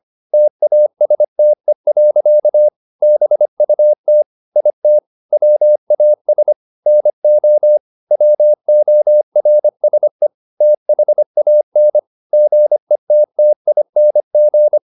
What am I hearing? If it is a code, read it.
Morse code, heard as a keyed tone, about 25 wpm, 600 Hz. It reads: TASTE. BUT IT WAS NO WORSE THAN GETTING
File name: the old man and the sea part 1 25wpm